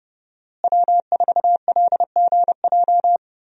Morse code, heard as a keyed tone, about 30 wpm, 700 Hz. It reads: W4LGJ
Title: USA Call Signs - 5 Characters - Repeated 2x - 30wpm